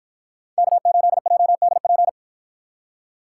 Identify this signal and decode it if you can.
Morse code, heard as a keyed tone, about 48 wpm, 700 Hz. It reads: K8JDP